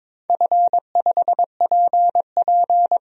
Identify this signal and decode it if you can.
Morse code, heard as a keyed tone, about 22 wpm, 700 Hz. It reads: F5PP
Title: DXCC-100 Call Signs - 4 Characters - 22wpm